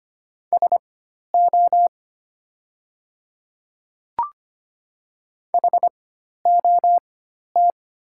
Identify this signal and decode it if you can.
Morse code, heard as a keyed tone, about 25 wpm, spaced as for 12 wpm, 700 Hz. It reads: SO HOT
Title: 4x Character Spacing - Top 100 Words - 25wpm